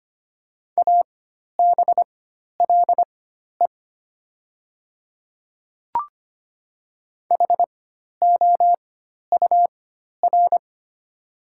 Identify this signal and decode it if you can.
Morse code, heard as a keyed tone, about 25 wpm, spaced as for 12 wpm, 700 Hz. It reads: ABLE HOUR
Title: 4x Character Spacing - Top 500 Words - 25wpm